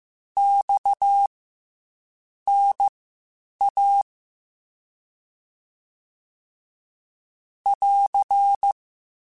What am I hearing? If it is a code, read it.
Morse code, heard as a keyed tone, about 15 wpm, spaced as for 7 wpm, 800 Hz. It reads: XNA <AR>
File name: CW Comparisons / CW Comparisons-mp3s / CW Comparisons Broken Characters @13